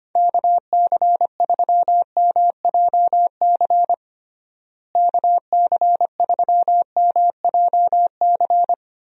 Morse code, heard as a keyed tone, about 25 wpm, 700 Hz. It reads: KC3MJC KC3MJC
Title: USA Call Signs - 6 Characters - Repeated 3x - 25wpm